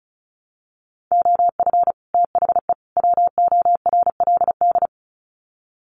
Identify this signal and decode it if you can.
Morse code, heard as a keyed tone, about 34 wpm, 700 Hz. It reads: OF THE WORLD
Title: Sets of 3 Words 35wpm